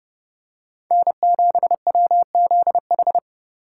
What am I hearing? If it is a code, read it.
Morse code, heard as a keyed tone, about 29 wpm, 700 Hz. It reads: N7WZH